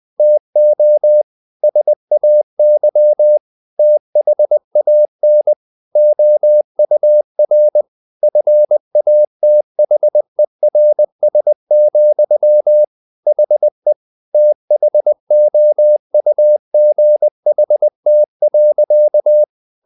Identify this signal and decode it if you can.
Morse code, heard as a keyed tone, about 20 wpm, 600 Hz. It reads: TO SAY THAN OUR FATHERS, HE THOUGHT.
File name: the old man and the sea / the old man and the sea part 1 20wpm